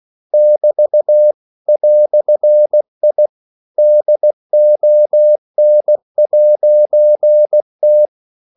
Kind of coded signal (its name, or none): Morse code